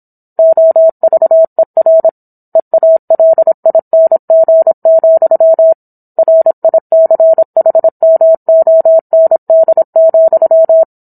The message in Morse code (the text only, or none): OVER EALING, RICHMOND,